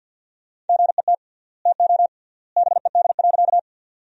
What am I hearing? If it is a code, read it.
Morse code, heard as a keyed tone, about 49 wpm, 700 Hz. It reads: GET TO BED.
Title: 3x Word-Spacing - Sentences from Top 500 Words - 50wpm